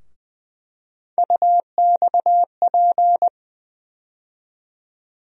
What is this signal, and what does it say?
Morse code, heard as a keyed tone, about 21 wpm, 700 Hz. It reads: UXP
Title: USA Call Sign Postfixes - 3 Letters 20wpm